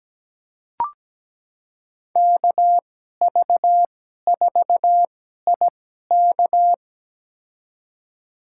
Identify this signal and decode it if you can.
Morse code, heard as a keyed tone, about 17 wpm, spaced as for 12 wpm, 700 Hz. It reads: KV4IK
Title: USA Call Signs - 5 Characters - 2x Character Spacing - 17wpm